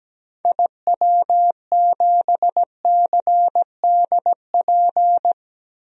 Morse code, heard as a keyed tone, about 17 wpm, 700 Hz. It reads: IW7CDP
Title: DXCC-100 Call Signs - 6 Characters - 17wpm